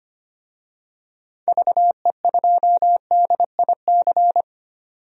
Morse code, heard as a keyed tone, about 25 wpm, 690 Hz. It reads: VE2DIC
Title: DXCC-100 Call Signs - 6 Characters - 25wpm